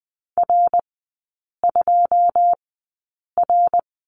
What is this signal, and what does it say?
Morse code, heard as a keyed tone, about 20 wpm, spaced as for 8 wpm, 700 Hz. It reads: R2R
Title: 081 Lesson 14e - Number 2 - Three Characters 20wpm